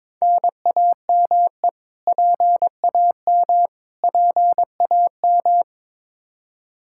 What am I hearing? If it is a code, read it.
Morse code, heard as a keyed tone, about 22 wpm, 700 Hz. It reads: NAME PAM PAM
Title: Practice QSO - Exchange 1 - Beginning - 22wpm